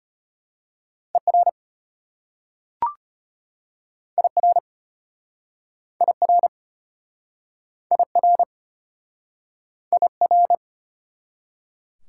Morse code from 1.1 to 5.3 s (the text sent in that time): ER IR